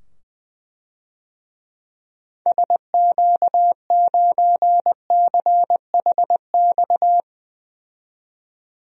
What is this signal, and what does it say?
Morse code, heard as a keyed tone, about 20 wpm, 700 Hz. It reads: SQ9CHX
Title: DXCC-100 Call Signs - 6 Characters - 20wpm